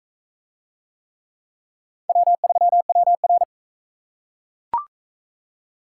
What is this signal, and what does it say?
Morse code, heard as a keyed tone, about 41 wpm, 700 Hz. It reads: W3WR